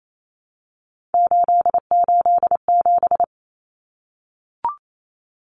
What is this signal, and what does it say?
Morse code, heard as a keyed tone, about 28 wpm, 700 Hz. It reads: 887
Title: Common Cut Numbers - 3-Digit 28wpm